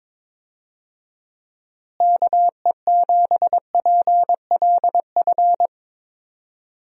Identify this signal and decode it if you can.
Morse code, heard as a keyed tone, about 22 wpm, 700 Hz. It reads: KE7PLF
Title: USA Call Signs - 6 Characters - Repeated 3x - 22wpm